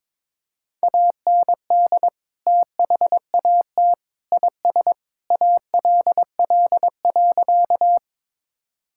Morse code, heard as a keyed tone, about 22 wpm, 700 Hz. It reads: AND THAT IS ALL.
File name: practice / 22wpm / Sentences with 3-Word Phrases - 22wpm